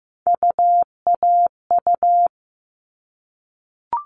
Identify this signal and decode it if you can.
Morse code, heard as a keyed tone, about 15 wpm, 700 Hz. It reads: UAU
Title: All Cut Numbers - 3-Digit 15wpm